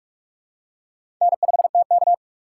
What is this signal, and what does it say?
Morse code, heard as a keyed tone, about 44 wpm, 700 Hz. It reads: N5TK
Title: USA Call Signs - 4 Characters - 6x Character Spacing - 45wpm